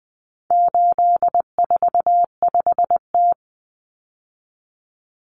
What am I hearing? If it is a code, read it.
Morse code, heard as a keyed tone, about 20 wpm, 700 Hz. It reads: 845T
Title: Common Cut Numbers - 4-Digit 20wpm